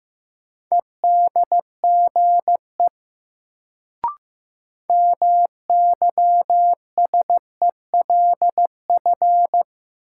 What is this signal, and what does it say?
Morse code, heard as a keyed tone, about 15 wpm, 700 Hz. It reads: EDGE MYSELF
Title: Sentences from Top 1300 Words - New Word Drill Alt - 15wpm